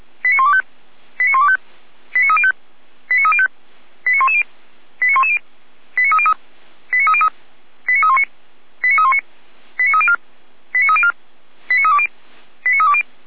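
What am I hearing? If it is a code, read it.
Five-tone selective call (ZVEI), heard as quick runs of five tones, tones of about 70 ms. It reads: 89125 89125 89305 89305 89110 89110 89303 89303 89319 89319 89304 89304 89320 89320
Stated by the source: ZVEI five-tone call received by SDR, from feuerwehr-orlamuende fire service audio archive.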